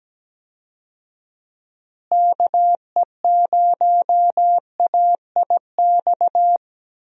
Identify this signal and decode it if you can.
Morse code, heard as a keyed tone, about 17 wpm, 700 Hz. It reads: KE0AIX
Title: USA Call Signs - 6 Characters - Repeated 2x - 17wpm